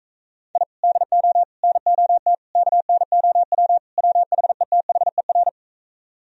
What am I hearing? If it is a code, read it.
Morse code, heard as a keyed tone, about 45 wpm, 700 Hz. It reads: I DO NOT KNOW WHETHER